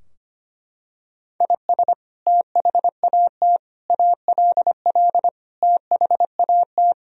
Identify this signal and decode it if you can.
Morse code, heard as a keyed tone, about 25 wpm, 700 Hz. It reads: IS THAT ALL THAT
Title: Sentences from Top 1000 Words - Review Part 3 - 25wpm